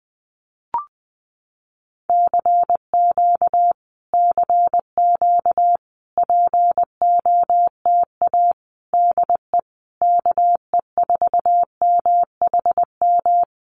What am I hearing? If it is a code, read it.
Morse code, heard as a keyed tone, about 20 wpm, 700 Hz. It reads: CQ CQ POTA DE KE4MHM
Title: Common QSO Phrases Encore 20wpm